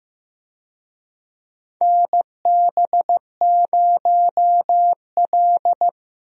Morse code, heard as a keyed tone, about 15 wpm, 700 Hz. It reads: NB0L
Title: USA Call Signs - 4 Characters - Repeated 2x - 15wpm